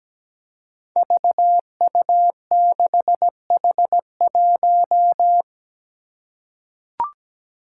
Morse code, heard as a keyed tone, about 17 wpm, 700 Hz. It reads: VU6H1
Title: Five Challenging Letter-Number Combinations 17wpm